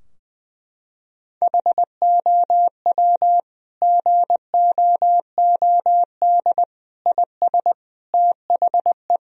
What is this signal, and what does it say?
Morse code, heard as a keyed tone, about 20 wpm, 700 Hz. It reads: HOW GOOD IS THE
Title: Sentences from Top 1000 Words - Review Part 4 - 20wpm